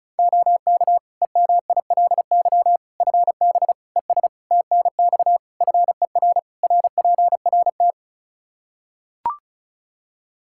Morse code, heard as a keyed tone, about 34 wpm, 700 Hz. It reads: OK EMILY FB ES TNX FER RPRT
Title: Practice QSO - Exchange 2 - Info - 35wpm